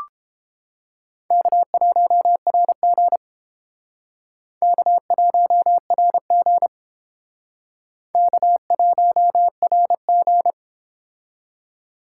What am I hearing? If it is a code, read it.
Morse code, heard as a keyed tone, about 29 wpm, 700 Hz. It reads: K1RG K1RG K1RG